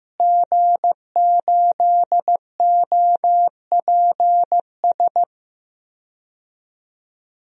Morse code, heard as a keyed tone, about 15 wpm, 700 Hz. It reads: G8OPS
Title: DXCC-100 Call Signs - 5 Characters - Repeated 2x - 15wpm